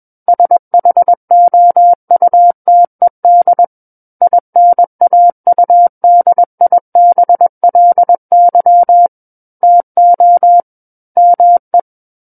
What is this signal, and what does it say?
Morse code, heard as a keyed tone, about 21 wpm, 700 Hz. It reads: SHOUTED INAUDIBLY TO ME
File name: Book One / 12-Chapter 12 - 21 WPM